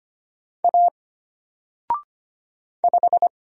Morse code, heard as a keyed tone, about 25 wpm, 700 Hz. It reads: A 5
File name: Single-Letter-Number-25wpm